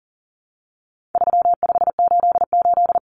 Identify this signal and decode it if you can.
Morse code, heard as a keyed tone, about 40 wpm, 700 Hz. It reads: 3588
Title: Common Cut Numbers - 4-Digit 40wpm